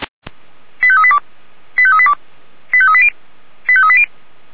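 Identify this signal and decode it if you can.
Five-tone selective call (ZVEI), heard as quick runs of five tones, tones of about 70 ms. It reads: 85382 85382 85380 85380
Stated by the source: Received on an SDR; ZVEI five-tone call 89155.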